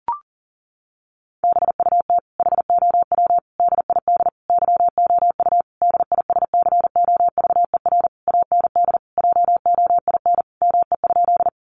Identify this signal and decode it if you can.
Morse code, heard as a keyed tone, about 40 wpm, 700 Hz. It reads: BUT HOW DID YOU DISCOVER AND JOIN ME?
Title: Sentences from Top 1400 Words 40wpm